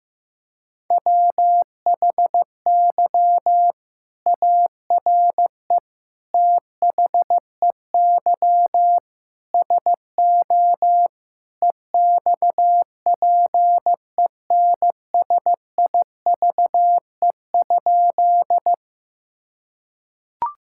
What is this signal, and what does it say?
Morse code, heard as a keyed tone, about 15 wpm, 700 Hz. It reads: WHY ARE THEY SO EXPENSIVE?
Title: Sentences from Top 1000 Words - Review Part 6 - 15wpm